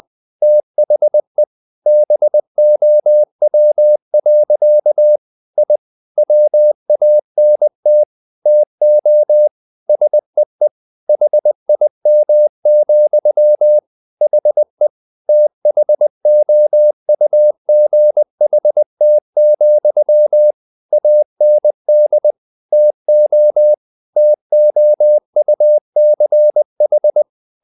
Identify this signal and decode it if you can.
Morse code, heard as a keyed tone, about 20 wpm, 600 Hz. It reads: THE BOW. I WANT TO SEE HIM, HE THOUGHT, AND TO TOUCH
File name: the old man and the sea part 2 20wpm